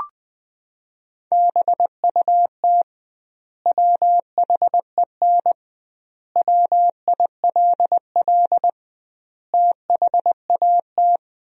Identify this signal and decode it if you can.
Morse code, heard as a keyed tone, about 20 wpm, 700 Hz. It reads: BUT WHEN WILL THAT
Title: 2x Word-Spacing - Sentences from Top 100 Words - 20wpm